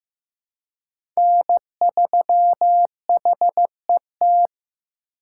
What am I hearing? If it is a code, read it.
Morse code, heard as a keyed tone, about 15 wpm, 700 Hz. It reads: N3HET
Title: USA Call Signs - 5 Characters - 2x Character Spacing - 15wpm